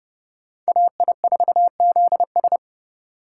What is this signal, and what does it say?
Morse code, heard as a keyed tone, about 36 wpm, 700 Hz. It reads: AI4ZS
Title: Speed Racing - USA 2x2 Call Signs 30wpm